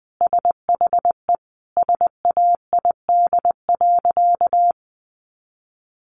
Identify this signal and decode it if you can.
Morse code, heard as a keyed tone, about 20 wpm, 700 Hz. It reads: SHE SAID.
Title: Sentences from Top 500 Words 20wpm